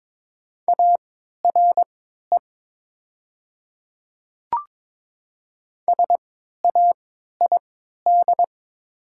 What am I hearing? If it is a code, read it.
Morse code, heard as a keyed tone, about 22 wpm, spaced as for 12 wpm, 700 Hz. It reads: ARE SAID
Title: 3x Character Spacing - Top 200 Words - 22wpm